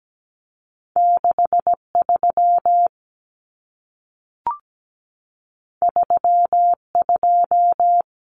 Morse code, heard as a keyed tone, about 17 wpm, 700 Hz. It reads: 63 32
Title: Common Cut Numbers - 2-Digit 17wpm